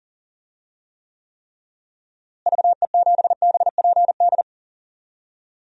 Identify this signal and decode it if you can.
Morse code, heard as a keyed tone, about 40 wpm, 700 Hz. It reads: VE7BPD
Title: DXCC-100 Call Signs - 6 Characters - Repeated 3x - 40wpm